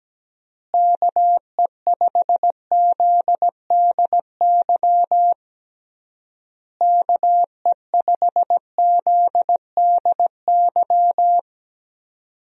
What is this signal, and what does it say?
Morse code, heard as a keyed tone, about 17 wpm, 700 Hz. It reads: KE5ZDY KE5ZDY
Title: USA Call Signs - 6 Characters - Repeated 3x - 17wpm